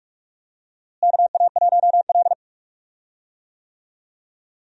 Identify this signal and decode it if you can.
Morse code, heard as a keyed tone, about 46 wpm, 700 Hz. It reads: KA1L
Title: USA Call Signs - 4 Characters - Repeated 2x - 45wpm